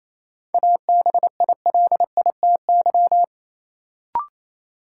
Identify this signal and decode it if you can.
Morse code, heard as a keyed tone, about 27 wpm, 700 Hz. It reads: ABILITY
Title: Sets of 3 Words - New Word Drill - 28wpm